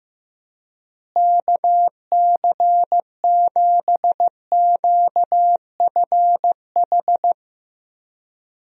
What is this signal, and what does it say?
Morse code, heard as a keyed tone, about 15 wpm, 700 Hz. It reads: KC7QFH
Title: USA Call Signs - 6 Characters - Repeated 3x - 15wpm